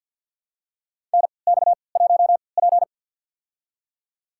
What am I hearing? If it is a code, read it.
Morse code, heard as a keyed tone, about 51 wpm, spaced as for 28 wpm, 700 Hz. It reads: NX1P
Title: USA Call Signs - 4 Characters - 3x Character Spacing - 50wpm